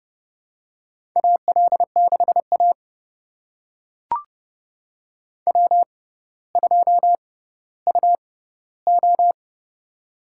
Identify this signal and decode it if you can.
Morse code, heard as a keyed tone, about 31 wpm, spaced as for 15 wpm, 700 Hz. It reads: AL6A W2UO
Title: USA Call Signs - 4 Characters - 6x Character Spacing - 30wpm